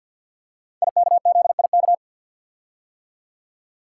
Morse code, heard as a keyed tone, about 47 wpm, 700 Hz. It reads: IKZIK
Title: Code Groups - 5 Letters - 50wpm